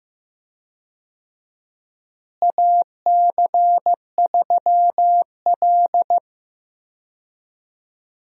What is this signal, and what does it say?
Morse code, heard as a keyed tone, about 15 wpm, 700 Hz. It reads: AC3L
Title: USA Call Signs - 4 Characters - Repeated 3x - 15wpm